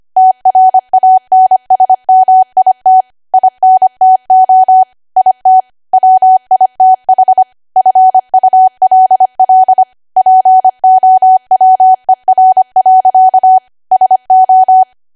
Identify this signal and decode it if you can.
Morse code, heard as a keyed tone, about 25 wpm, 750 Hz. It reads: TRANSMIT INTO IT WITH FULL POWER. SO
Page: 25 WPM Code Archive